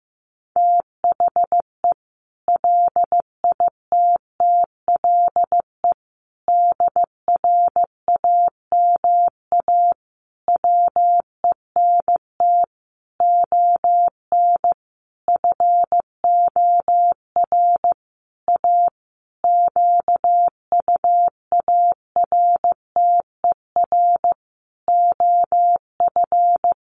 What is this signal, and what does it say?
Morse code, heard as a keyed tone, about 15 wpm, 700 Hz. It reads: THE LITTLE DRAMA WENT ON FOR A QUARTER OF
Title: Sentences from Top 1500 Words 15wpm